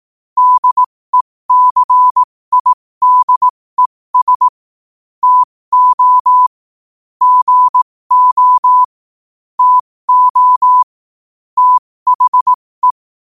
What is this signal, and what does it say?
Morse code, heard as a keyed tone, about 18 wpm, spaced as for 15 wpm, 1000 Hz. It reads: DECIDES TO GO TO THE
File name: qotdfiles / 15wpm